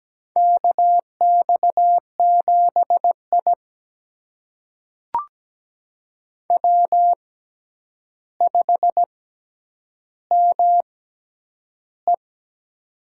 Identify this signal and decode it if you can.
Morse code, heard as a keyed tone, about 17 wpm, spaced as for 9 wpm, 700 Hz. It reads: KX7I W5ME
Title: USA Call Signs - 4 Characters - 6x Character Spacing - 17wpm